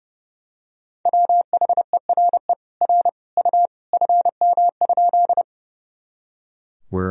Morse code, heard as a keyed tone, about 32 wpm, 700 Hz. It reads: WHERE R U FM?
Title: Sentences from Top 200 Words and Abbreviations 30wpm